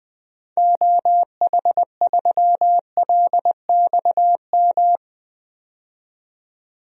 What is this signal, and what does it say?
Morse code, heard as a keyed tone, about 20 wpm, 700 Hz. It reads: OH3LXM
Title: DXCC-100 Call Signs - 6 Characters - 20wpm